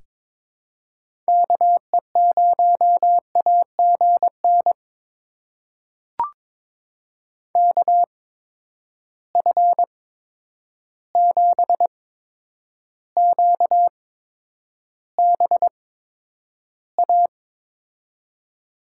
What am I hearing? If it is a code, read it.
Morse code, heard as a keyed tone, about 22 wpm, spaced as for 9 wpm, 700 Hz. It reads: KE0AGN KF7QBA
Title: USA Call Signs - 8x Character Spacing - 22wpm